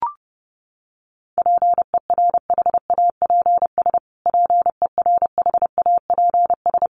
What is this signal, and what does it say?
Morse code, heard as a keyed tone, about 30 wpm, 700 Hz. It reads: PERHAPS PERHAPS
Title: Top 500 Words Repeated 30wpm